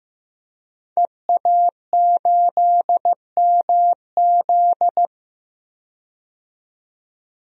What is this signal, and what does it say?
Morse code, heard as a keyed tone, about 15 wpm, 700 Hz. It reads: EA8MZ
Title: DXCC-100 Call Signs - 5 Characters - 15wpm